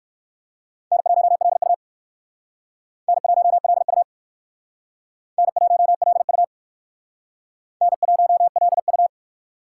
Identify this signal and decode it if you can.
Morse code, heard as a keyed tone, about 58 wpm, 700 Hz. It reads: N1LU N1LU N1LU N1LU